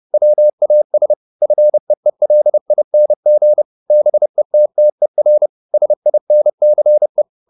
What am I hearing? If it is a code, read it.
Morse code, heard as a keyed tone, about 30 wpm, 600 Hz. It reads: WAS FEELING BETTER SINCE